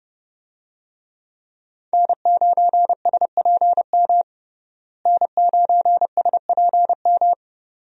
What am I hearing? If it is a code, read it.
Morse code, heard as a keyed tone, about 32 wpm, 700 Hz. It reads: N9SPM N9SPM
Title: USA Call Signs - 5 Characters - Repeated 2x - 30wpm